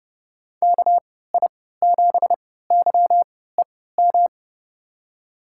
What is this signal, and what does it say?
Morse code, heard as a keyed tone, about 30 wpm, spaced as for 17 wpm, 700 Hz. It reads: KI7YEM
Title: USA Call Signs - 6 Characters - 3x Character Spacing - 30wpm